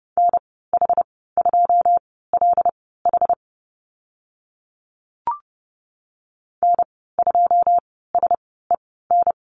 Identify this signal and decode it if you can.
Morse code, heard as a keyed tone, about 30 wpm, spaced as for 17 wpm, 700 Hz. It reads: NH2LH N2SEN
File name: Lesson 15g - Number 5 - Call Signs 30wpm